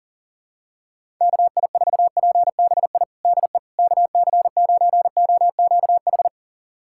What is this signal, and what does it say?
Morse code, heard as a keyed tone, about 40 wpm, 700 Hz. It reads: KI4PBI DE KC9OQH
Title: USA Callsign DE Callsign 40wpm